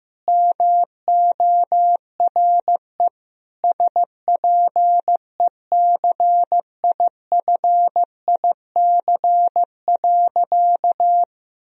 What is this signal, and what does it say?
Morse code, heard as a keyed tone, about 16 wpm, 700 Hz. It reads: MORE SPECIFIC.
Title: Sentences from Top 1000 Words - Review Part 3 - 15wpm